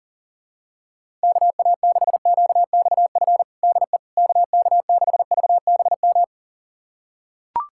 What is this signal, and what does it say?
Morse code, heard as a keyed tone, about 40 wpm, 700 Hz. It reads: KA6QXF DE KK6VBM
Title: USA Callsign DE Callsign - 6 Characters - 40wpm